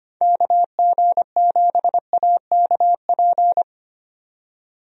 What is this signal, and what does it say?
Morse code, heard as a keyed tone, about 25 wpm, 700 Hz. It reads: KG7AKP
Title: USA Call Signs - 6 Characters - 25wpm